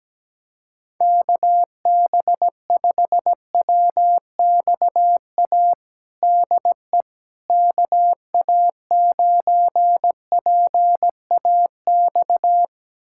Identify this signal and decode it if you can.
Morse code, heard as a keyed tone, about 17 wpm, 700 Hz. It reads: KB5WXA DE KA9PAX
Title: USA Callsign DE Callsign - 6 Characters - 17wpm